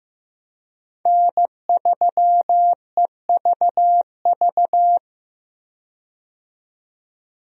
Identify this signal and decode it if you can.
Morse code, heard as a keyed tone, about 15 wpm, 700 Hz. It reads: N3EVV
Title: USA Call Signs - 5 Characters - 15wpm